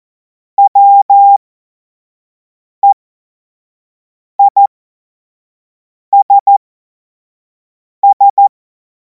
Morse code, heard as a keyed tone, about 14 wpm, spaced as for 5 wpm, 800 Hz. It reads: WEISS
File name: Morsekurs / Witze / Witz 17. März 2026 25 F70